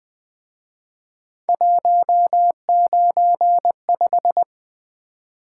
Morse code, heard as a keyed tone, about 24 wpm, 700 Hz. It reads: 195
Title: Speed-Racing - Three Letter-Number Combinations - 20wpm